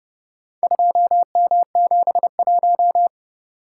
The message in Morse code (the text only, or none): 2M71